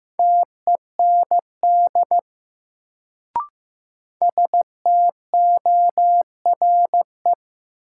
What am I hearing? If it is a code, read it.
Morse code, heard as a keyed tone, about 15 wpm, 700 Hz. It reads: TEND STORE
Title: Sentences from Top 1000 Words - Review Part 5 - New Word Drill - 15wpm